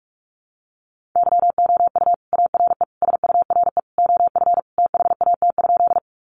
Morse code, heard as a keyed tone, about 44 wpm, 700 Hz. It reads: YOU ARE SURE OF THAT?
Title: Sentences from Top 300 Words 45wpm